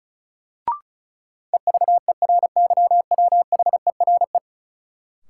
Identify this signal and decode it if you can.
Morse code, heard as a keyed tone, about 36 wpm, 700 Hz. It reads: EVERYWHERE
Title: Sentences from Top 1600 Words - New Word Drill - 35wpm